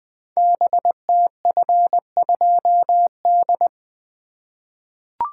Morse code, heard as a keyed tone, about 20 wpm, 700 Hz. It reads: BTF2D
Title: Code Groups - 5 Letter:Numbers - 20wpm